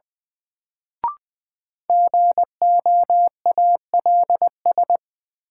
Morse code, heard as a keyed tone, about 20 wpm, 700 Hz. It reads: GOALS
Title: Sentences from Top 1000 Words - Review Part 6 - New Word Drill - 20wpm